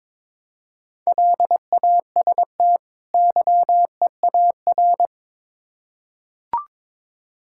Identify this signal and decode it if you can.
Morse code, heard as a keyed tone, about 22 wpm, 700 Hz. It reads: LAST YEAR
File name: Build A Phrase - 2-Word Phrases - 22wpm